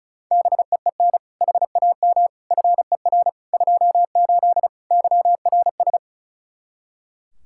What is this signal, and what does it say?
Morse code, heard as a keyed tone, about 36 wpm, 700 Hz. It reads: BEEN HAM FER 28 YRS
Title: Practice QSO - Exchange 3 - More Info - 35wpm